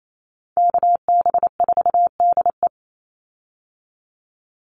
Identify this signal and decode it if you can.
Morse code, heard as a keyed tone, about 28 wpm, 700 Hz. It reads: KB4DE
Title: Call Signs - 28wpm